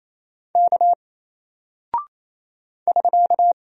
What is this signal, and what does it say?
Morse code, heard as a keyed tone, about 27 wpm, 700 Hz. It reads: K <SK>
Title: Single Character (Letter:Number:Punctuation:Prosign) 28wpm